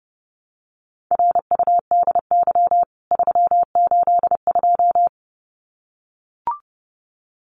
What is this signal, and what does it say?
Morse code, heard as a keyed tone, about 29 wpm, 700 Hz. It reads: RUDY 382
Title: CWT Exchange 30wpm